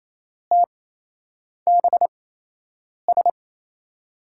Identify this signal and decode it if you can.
Morse code, heard as a keyed tone, about 28 wpm, spaced as for 8 wpm, 700 Hz. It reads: TBS
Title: USA Call Signs - 6 Characters - 8x Character Spacing - 28wpm